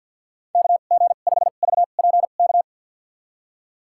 Morse code, heard as a keyed tone, about 53 wpm, spaced as for 37 wpm, 700 Hz. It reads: KG5VPK
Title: USA Call Signs - 2x Character Spacing - 50wpm